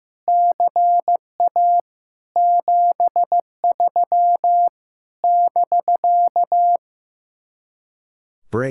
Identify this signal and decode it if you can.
Morse code, heard as a keyed tone, about 15 wpm, 700 Hz. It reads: CA 73 <BK>
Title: Speed-Racing Common QSO Phrases Encore - 15wpm